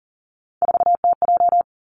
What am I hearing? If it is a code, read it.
Morse code, heard as a keyed tone, about 40 wpm, 700 Hz. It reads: 4TJ
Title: Three Challenging Letter-Number Combinations 40wpm